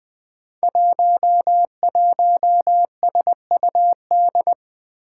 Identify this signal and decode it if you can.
Morse code, heard as a keyed tone, about 20 wpm, 700 Hz. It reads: 11SUD
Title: Five Challenging Letter-Number Combinations 20wpm